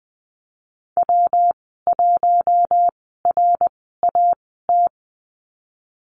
Morse code, heard as a keyed tone, about 20 wpm, spaced as for 15 wpm, 700 Hz. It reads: W1RAT
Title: 154 Lesson 25g - Letter p - Call Signs 20wpm